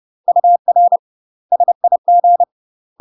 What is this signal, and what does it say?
Morse code, heard as a keyed tone, about 30 wpm, 700 Hz. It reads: UR SIG